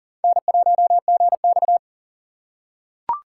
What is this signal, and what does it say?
Morse code, heard as a keyed tone, about 40 wpm, 700 Hz. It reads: N1GX
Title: USA Call Signs - 3x Character Spacing - 40wpm